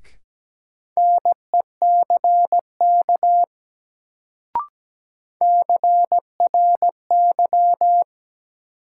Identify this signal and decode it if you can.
Morse code, heard as a keyed tone, about 17 wpm, 700 Hz. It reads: NECK CRY
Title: Sets of 5 Words Encore - New Word Drill - 17wpm